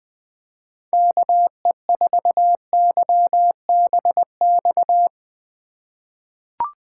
Morse code, heard as a keyed tone, about 20 wpm, 700 Hz. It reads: KE4YBX
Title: USA Call Signs - 8x Character Spacing - 20wpm